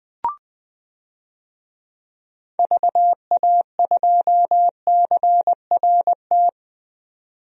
Morse code, heard as a keyed tone, about 20 wpm, 700 Hz. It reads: VA2CRT
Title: DXCC-100 Call Signs - 6 Characters - 20wpm